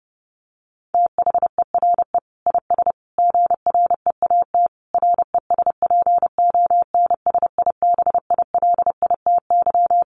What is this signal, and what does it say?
Morse code, heard as a keyed tone, about 30 wpm, 700 Hz. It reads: THERE IS GREAT RESPONSIBILITY
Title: Sentences from Top 1200 Words 30wpm